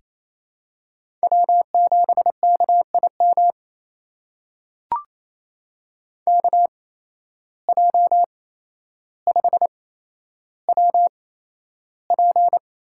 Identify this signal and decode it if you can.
Morse code, heard as a keyed tone, about 27 wpm, spaced as for 12 wpm, 700 Hz. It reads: W7KIM KJ5WP
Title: USA Call Signs - 5 Characters - 8x Character Spacing - 28wpm